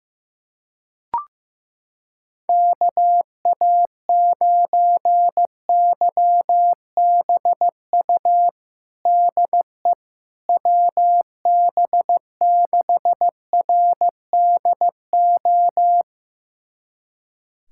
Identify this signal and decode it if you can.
Morse code, heard as a keyed tone, about 15 wpm, 700 Hz. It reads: KA9YBU DE WB6RDO
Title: USA Callsign DE Callsign 15wpm